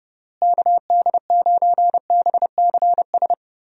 Morse code, heard as a keyed tone, about 30 wpm, 700 Hz. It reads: KD9BCS
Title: USA Call Signs - 6 Characters - Repeated 3x - 30wpm